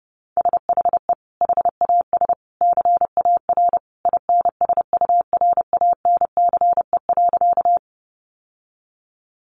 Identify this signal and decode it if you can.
Morse code, heard as a keyed tone, about 30 wpm, 700 Hz. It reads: SHE HAS CAR INSURANCE.
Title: Sentences from Top 1100 Words 30wpm